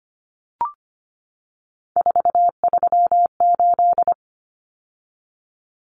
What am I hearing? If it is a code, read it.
Morse code, heard as a keyed tone, about 25 wpm, 700 Hz. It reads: 438
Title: Common Cut Numbers - 3-Digit 25wpm